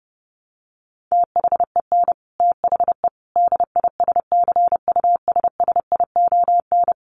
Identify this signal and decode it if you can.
Morse code, heard as a keyed tone, about 30 wpm, 700 Hz. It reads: THEN THE DISCUSSION
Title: Sentences from Top 1100 Words 30wpm